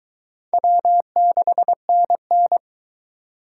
Morse code, heard as a keyed tone, about 23 wpm, 700 Hz. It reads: W6NN